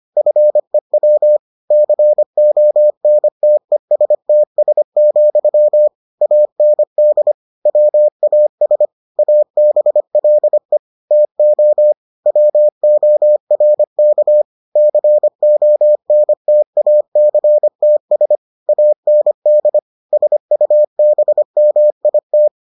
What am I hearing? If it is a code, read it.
Morse code, heard as a keyed tone, about 25 wpm, 600 Hz. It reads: FEW CONTESTS, AND WAS ABLE TO WORK CONTACTS AND SUBMIT